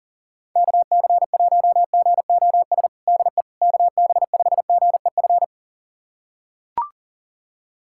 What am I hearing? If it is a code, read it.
Morse code, heard as a keyed tone, about 40 wpm, 700 Hz. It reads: KC1GOS DE KB5GEF
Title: USA Callsign DE Callsign - 6 Characters - 40wpm